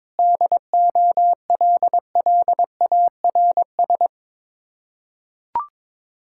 Morse code, heard as a keyed tone, about 22 wpm, 700 Hz. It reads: DOLLARS
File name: practice / 22wpm / Build A Phrase - 5-Word Phrases - 22wpm